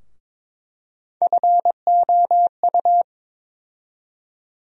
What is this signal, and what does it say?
Morse code, heard as a keyed tone, about 22 wpm, 700 Hz. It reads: FOU
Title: USA Call Sign Postfixes - 3 Letters 22wpm